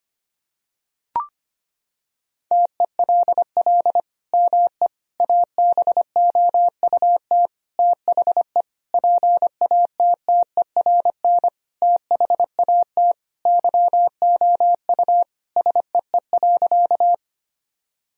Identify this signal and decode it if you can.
Morse code, heard as a keyed tone, about 25 wpm, 700 Hz. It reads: TELL ME ABOUT THE PATTERN THAT YOU SEE.
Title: Sentences from Top 1000 Words - Review Part 2 - 25wpm